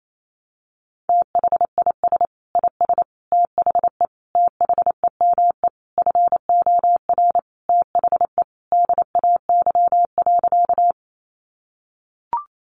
Morse code, heard as a keyed tone, about 28 wpm, 700 Hz. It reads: THIS IS THE THEME FOR THE DAY.
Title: Sentences from Top 1100 Words 28wpm